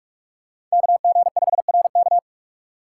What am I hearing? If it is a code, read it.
Morse code, heard as a keyed tone, about 42 wpm, 700 Hz. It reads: KG5RK
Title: USA Call Signs - 5 Characters - 3x Character Spacing - 45wpm